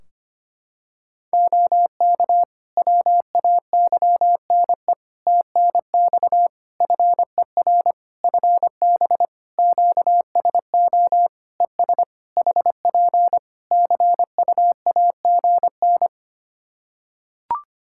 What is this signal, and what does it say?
Morse code, heard as a keyed tone, about 25 wpm, 700 Hz. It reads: OK WAYNE TNX FER FB QSO ES HP CUAGN
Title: Practice QSO - Exchange 4 - Ending - 25wpm